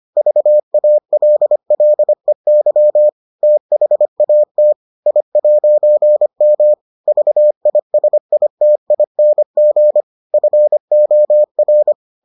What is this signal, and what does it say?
Morse code, heard as a keyed tone, about 25 wpm, 600 Hz. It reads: VALLEY THAT I'M VISITING FOR